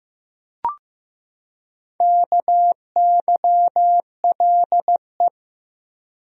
Morse code, heard as a keyed tone, about 15 wpm, 700 Hz. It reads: KYLE
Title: USA Common Names - Max 4 Letters - 15wpm